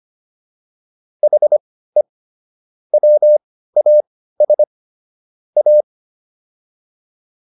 Morse code, heard as a keyed tone, about 25 wpm, spaced as for 15 wpm, 600 Hz. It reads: HE WAS A